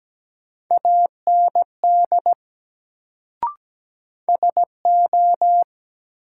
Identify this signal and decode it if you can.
Morse code, heard as a keyed tone, about 17 wpm, 700 Hz. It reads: AND SO
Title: Sentences from Top 100 Words - New Word Drill - 17wpm